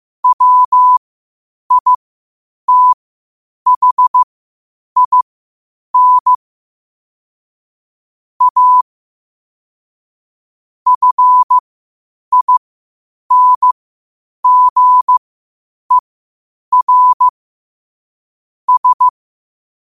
Morse code, heard as a keyed tone, about 15 wpm, spaced as for 8 wpm, 1000 Hz. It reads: WITHIN A FINGER S
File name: qotdfiles / 8wpm